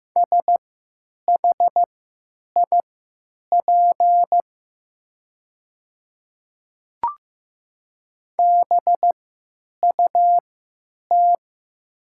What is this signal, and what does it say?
Morse code, heard as a keyed tone, about 15 wpm, spaced as for 9 wpm, 700 Hz. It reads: SHIP BUT
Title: 3x Character Spacing - Top 500 Words - 15wpm